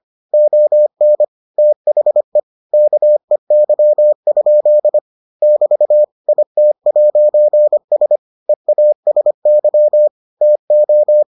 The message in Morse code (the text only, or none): ON THE KEY? <BT> IT'S EASY TO